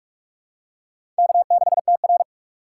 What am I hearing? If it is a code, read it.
Morse code, heard as a keyed tone, about 46 wpm, 700 Hz. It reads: K6TR
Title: USA Call Signs - 4 Characters - 8x Character Spacing - 45wpm